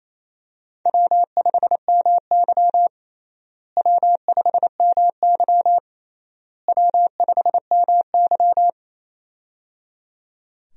Morse code, heard as a keyed tone, about 28 wpm, 700 Hz. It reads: W5MY W5MY W5MY
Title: USA Call Signs - 4 Characters - Repeated 3x - 28wpm